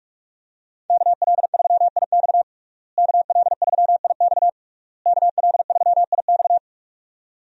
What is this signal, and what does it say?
Morse code, heard as a keyed tone, about 41 wpm, 700 Hz. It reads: KL3IX KL3IX KL3IX
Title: USA Call Signs - 5 Characters - Repeated 3x - 45wpm